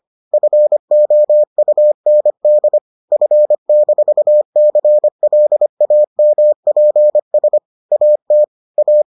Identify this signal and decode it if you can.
Morse code, heard as a keyed tone, about 25 wpm, 600 Hz. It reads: FOUND F-CLAMPS AT A